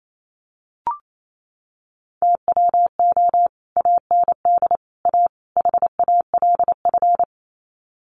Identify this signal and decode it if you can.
Morse code, heard as a keyed tone, about 28 wpm, 700 Hz. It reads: TWO AND A HALF
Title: Sets of 4 Words 28wpm